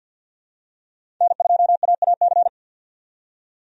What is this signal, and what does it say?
Morse code, heard as a keyed tone, about 46 wpm, 700 Hz. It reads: N2AAC
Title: USA Call Signs - 5 Characters - 50wpm